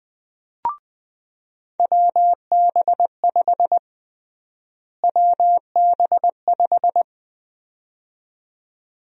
Morse code, heard as a keyed tone, about 21 wpm, 700 Hz. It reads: WB5 WB5
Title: USA Call Sign Prefixes - Repeated 2x - 20wpm